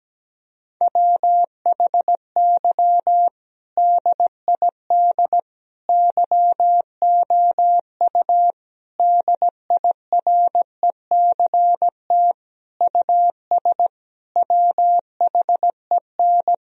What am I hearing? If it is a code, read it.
Morse code, heard as a keyed tone, about 17 wpm, 700 Hz. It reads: WHY DID YOU DIRECT US WHEN
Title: Sentences from Top 1000 Words - Review Part 2 - 17wpm